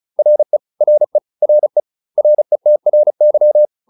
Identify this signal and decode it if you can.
Morse code, heard as a keyed tone, about 34 wpm, 600 Hz. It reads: RE RE RE RETRY